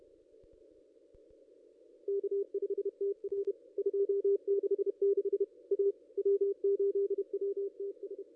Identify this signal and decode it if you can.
Morse code, heard as a keyed tone, about 31 wpm, 440 Hz. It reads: K5TR 266 A W8WTS
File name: contests / ss-cw-k5tr-op-n5rz -0100